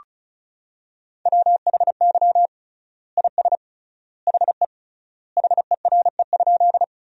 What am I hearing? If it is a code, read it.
Morse code, heard as a keyed tone, about 36 wpm, 700 Hz. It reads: WHY IS HE HERE?
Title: 3x Word-Spacing - Sentences from Top 200 Words - 35wpm